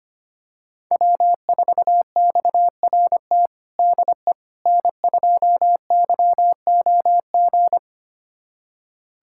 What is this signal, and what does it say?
Morse code, heard as a keyed tone, about 25 wpm, 700 Hz. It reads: W4XRT DE N2YOG
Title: USA Callsign DE Callsign 25wpm